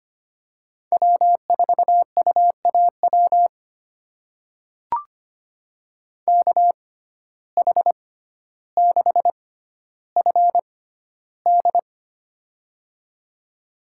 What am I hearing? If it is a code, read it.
Morse code, heard as a keyed tone, about 25 wpm, spaced as for 13 wpm, 700 Hz. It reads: W4UAW KH6FD
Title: USA Call Signs - 5 Characters - 6x Character Spacing - 25wpm